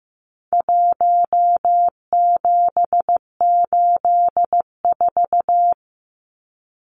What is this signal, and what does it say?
Morse code, heard as a keyed tone, about 15 wpm, 700 Hz. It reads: 1784
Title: Numbers - 4-Digit 15wpm